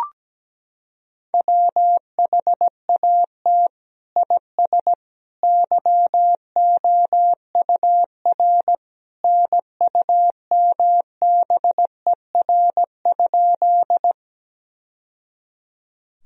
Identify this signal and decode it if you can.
Morse code, heard as a keyed tone, about 17 wpm, 700 Hz. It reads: WHAT IS YOUR NUMBER?
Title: Sentences from Top 1000 Words - Review Part 1 - 17wpm